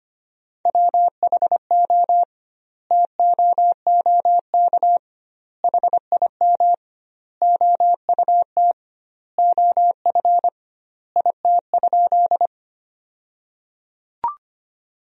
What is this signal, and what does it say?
Morse code, heard as a keyed tone, about 25 wpm, 700 Hz. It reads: WHO TOOK HIM OUT OF IT?
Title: 2x Word-Spacing - Sentences from Top 300 Words - 25wpm